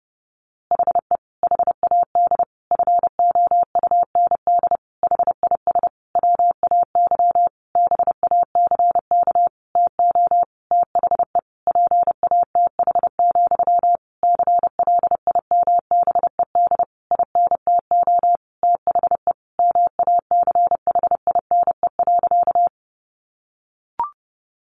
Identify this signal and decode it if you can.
Morse code, heard as a keyed tone, about 28 wpm, 700 Hz. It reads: HE HAD FOUND HIS WAY BACK TO THE PATH, CLIMBED INTO THE MACHINE.
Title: A Sound Of Thunder 30wpm